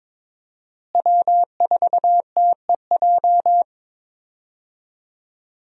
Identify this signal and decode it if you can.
Morse code, heard as a keyed tone, about 22 wpm, 700 Hz. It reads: W4TEJ
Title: USA Call Signs - 5 Characters - Repeated 3x - 22wpm